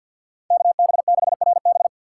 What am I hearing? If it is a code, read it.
Morse code, heard as a keyed tone, about 52 wpm, 700 Hz. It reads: KB6RB
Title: USA Call Signs - 5 Characters - 6x Character Spacing - 50wpm